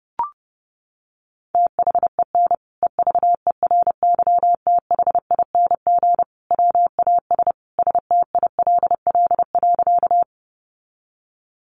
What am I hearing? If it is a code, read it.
Morse code, heard as a keyed tone, about 30 wpm, 700 Hz. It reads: THEN EVERYTHING WAS STILL.
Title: Sentences from Top 700 Words 30wpm